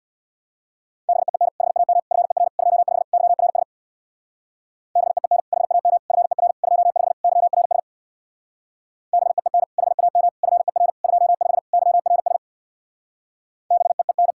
Morse code, heard as a keyed tone, about 66 wpm, 700 Hz. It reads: BEEN HAM FER 25 YRS BEEN HAM FER 25 YRS BEEN HAM FER 25 YRS BEEN